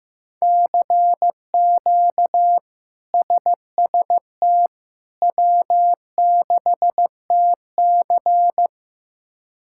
Morse code, heard as a keyed tone, about 15 wpm, 700 Hz. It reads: CQ SST W6TC
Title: K1USN SST QSO Practice - 15wpm